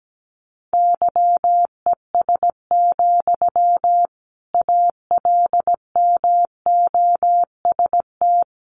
Morse code, heard as a keyed tone, about 17 wpm, 700 Hz. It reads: YES, ALMOST
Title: Sentences from Top 700 Words 17wpm